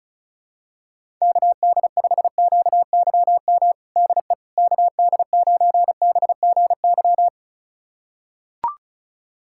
Morse code, heard as a keyed tone, about 34 wpm, 700 Hz. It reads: KD5QYM DE KD9BGY
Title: USA Callsign DE Callsign - 6 Characters - 35wpm